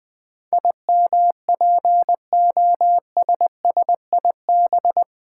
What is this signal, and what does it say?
Morse code, heard as a keyed tone, about 20 wpm, 700 Hz. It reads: IMPOSSIB
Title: Build A Word - 4 Letters At A Time - 10-Letter Words - 20wpm